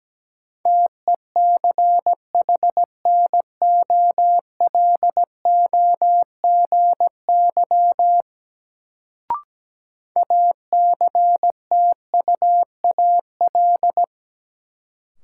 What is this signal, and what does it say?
Morse code, heard as a keyed tone, about 17 wpm, 700 Hz. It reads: TECHNOLOGY ACTUAL
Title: Sentences from Top 1000 Words - Review Part 5 - New Word Drill - 17wpm